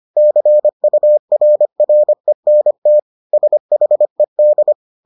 Morse code, heard as a keyed tone, about 25 wpm, 600 Hz. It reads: CURRENT SHED